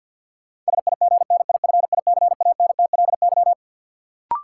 Morse code, heard as a keyed tone, about 52 wpm, 700 Hz. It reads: SIGNIFICANTLY
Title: Sentences from Top 1500 Words - New Word Drill - 50wpm